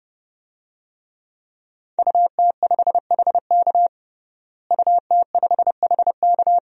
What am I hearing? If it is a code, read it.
Morse code, heard as a keyed tone, about 30 wpm, 700 Hz. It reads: UT5HK UT5HK
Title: DXCC-100 Call Signs - 5 Characters - Repeated 2x - 30wpm